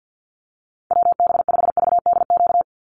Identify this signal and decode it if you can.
Morse code, heard as a keyed tone, about 50 wpm, 700 Hz. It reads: WB5VDQ
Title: USA Call Signs 50wpm